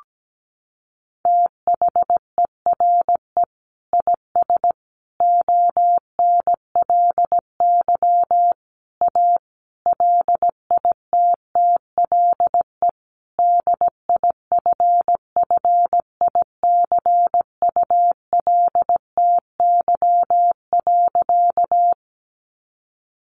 Morse code, heard as a keyed tone, about 17 wpm, 700 Hz. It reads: THERE IS ONLY A LITTLE DIFFICULTY.
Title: Sentences from Top 1500 Words 17wpm